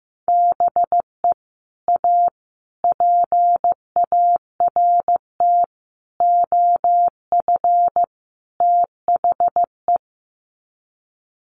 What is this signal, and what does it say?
Morse code, heard as a keyed tone, about 15 wpm, 700 Hz. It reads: BE A PART OF THE
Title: Sets of 5 Words Encore 15wpm